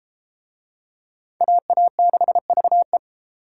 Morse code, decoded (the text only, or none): AA6VE